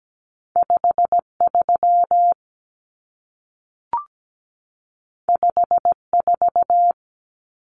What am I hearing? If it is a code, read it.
Morse code, heard as a keyed tone, about 17 wpm, 700 Hz. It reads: 53 54
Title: Numbers - 2-Digit 17wpm